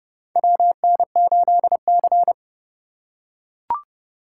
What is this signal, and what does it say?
Morse code, heard as a keyed tone, about 32 wpm, 700 Hz. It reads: WN8C
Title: USA Call Signs - 4 Characters - 3x Character Spacing - 30wpm